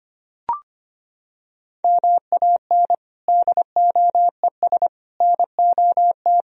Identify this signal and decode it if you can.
Morse code, heard as a keyed tone, about 25 wpm, 700 Hz. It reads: MAN DOES NOT
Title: Sentences from Top 1000 Words - Review Part 3 - 25wpm